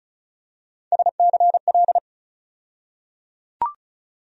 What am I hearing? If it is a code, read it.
Morse code, heard as a keyed tone, about 36 wpm, 700 Hz. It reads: SCL
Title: USA Call Sign Postfixes - 3 Letters 35wpm